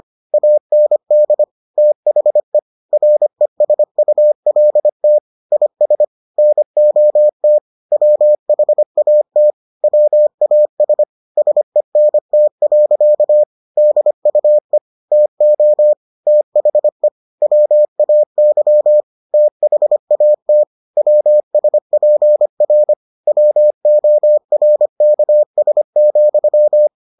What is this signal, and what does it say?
Morse code, heard as a keyed tone, about 25 wpm, 600 Hz. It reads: AND THE RESULT IS NOT WHAT WAS SENT. DUE TO THE WAY THAT WSPR WORKS,